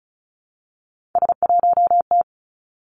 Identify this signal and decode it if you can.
Morse code, heard as a keyed tone, about 36 wpm, 700 Hz. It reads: S1T
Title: Three Challenging Letter-Number Combinations 35wpm